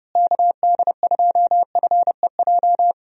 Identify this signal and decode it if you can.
Morse code, heard as a keyed tone, about 31 wpm, 700 Hz. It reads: KD2FEJ
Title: USA Call Signs - 6 Characters - Repeated 2x - 30wpm